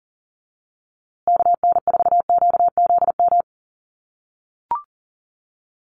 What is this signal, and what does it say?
Morse code, heard as a keyed tone, about 37 wpm, 700 Hz. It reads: KN4QZM
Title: USA Call Signs Repeated 3x 40wpm